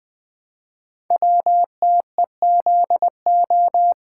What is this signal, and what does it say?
Morse code, heard as a keyed tone, about 20 wpm, 700 Hz. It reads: WTEZO
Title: Code Groups - 5 Letters - 20wpm